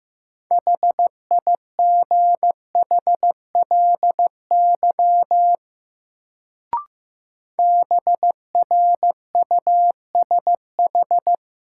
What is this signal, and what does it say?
Morse code, heard as a keyed tone, about 15 wpm, 700 Hz. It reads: HIGHLY BRUSH
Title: Sentences from Top 1100 Words - New Word Drill - 15wpm